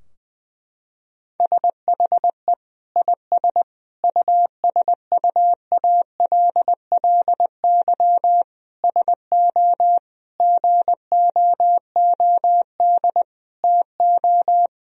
Morse code, heard as a keyed tone, about 20 wpm, 700 Hz. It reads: SHE IS USUALLY SO GOOD TO
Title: Sentences from Top 1000 Words - Review Part 3 - 20wpm